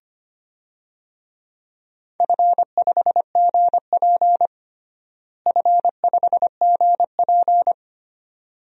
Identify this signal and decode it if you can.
Morse code, heard as a keyed tone, about 25 wpm, 700 Hz. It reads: F5GP F5GP
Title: DXCC-100 Call Signs - 4 Characters - Repeated 2x - 25wpm